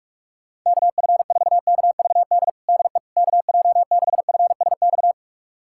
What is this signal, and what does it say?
Morse code, heard as a keyed tone, about 46 wpm, 700 Hz. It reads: KF4KVD DE KJ6FSX
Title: USA Callsign DE Callsign - 6 Characters - 45wpm